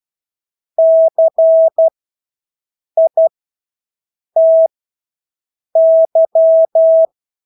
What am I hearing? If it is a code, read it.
Morse code, heard as a keyed tone, about 12 wpm, spaced as for 6 wpm, 650 Hz. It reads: CITY
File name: Engelske ord 12-6 wpm